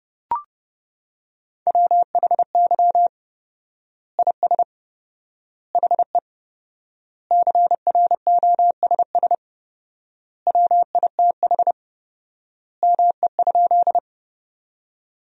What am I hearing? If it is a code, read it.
Morse code, heard as a keyed tone, about 30 wpm, 700 Hz. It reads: WHY IS HE CROSS WITH ME?
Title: 4x Word-Spacing - Sentences from Top 300 Words - 30wpm